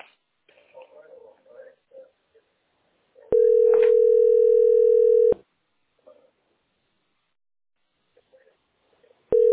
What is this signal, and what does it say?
Ringback tone (North America), 440 + 480 Hz, with speech.
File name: 5291aeb3-6831-42dd-95e5